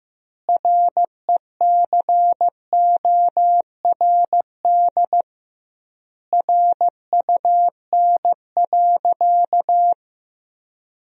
Morse code, heard as a keyed tone, about 15 wpm, 700 Hz. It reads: RECORD RUN.
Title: 2x Word-Spacing - Sentences from Top 500 Words - 15wpm